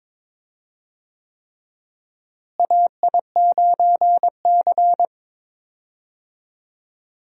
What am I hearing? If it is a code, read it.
Morse code, heard as a keyed tone, about 21 wpm, 700 Hz. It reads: AI9C
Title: USA Call Signs - 4 Characters - Repeated 3x - 22wpm